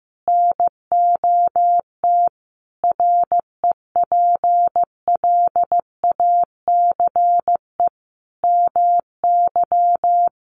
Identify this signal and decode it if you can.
Morse code, heard as a keyed tone, about 15 wpm, 700 Hz. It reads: NOT REPLACE MY
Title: Sentences from Top 1400 Words 15wpm